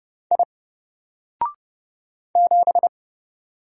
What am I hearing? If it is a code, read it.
Morse code, heard as a keyed tone, about 30 wpm, 700 Hz. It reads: I 7
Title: Single Character (Letter:Number:Punctuation:Prosign) 30wpm